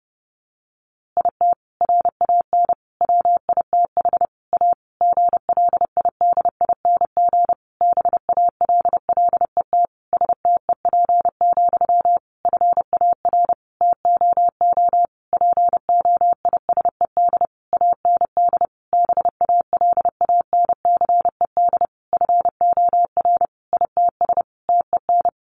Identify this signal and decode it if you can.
Morse code, heard as a keyed tone, about 30 wpm, 700 Hz. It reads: IT RAN WITH A GLIDING BALLET STEP, FAR TOO POISED AND BALANCED FOR ITS TEN
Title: A Sound Of Thunder 30wpm